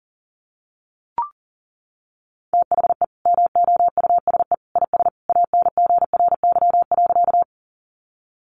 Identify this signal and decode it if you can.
Morse code, heard as a keyed tone, about 37 wpm, 700 Hz. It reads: THE MOUSE IS ANGRY.
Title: Easy Phrases 40wpm